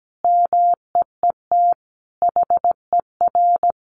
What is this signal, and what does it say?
Morse code, heard as a keyed tone, about 17 wpm, 700 Hz. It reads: MEET HER
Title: Sentences from Top 1100 Words 17wpm